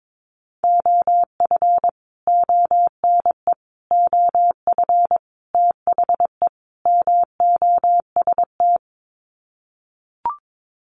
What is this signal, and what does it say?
Morse code, heard as a keyed tone, about 22 wpm, 700 Hz. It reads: OF ONE OF THE MOST
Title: Sets of 5 Words Encore2 22wpm